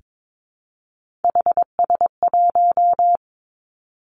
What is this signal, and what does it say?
Morse code, heard as a keyed tone, about 22 wpm, 700 Hz. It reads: HS1
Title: Three Challenging Letter-Number Combinations 22wpm